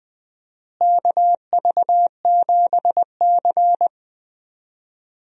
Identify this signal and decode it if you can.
Morse code, heard as a keyed tone, about 20 wpm, 700 Hz. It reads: KV7C
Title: USA Call Signs - 4 Characters - Repeated 3x - 20wpm